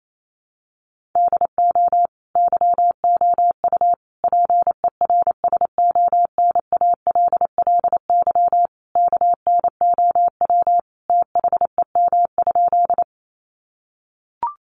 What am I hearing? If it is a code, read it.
Morse code, heard as a keyed tone, about 28 wpm, 700 Hz. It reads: DO YOU PERSONALLY KNOW THEM?
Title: Sentences from Top 1400 Words 28wpm